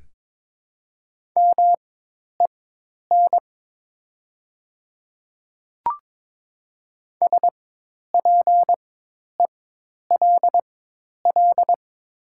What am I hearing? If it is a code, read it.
Morse code, heard as a keyed tone, about 22 wpm, spaced as for 10 wpm, 700 Hz. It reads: MEN SPELL
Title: 4x Character Spacing - Top 200 Words - 22wpm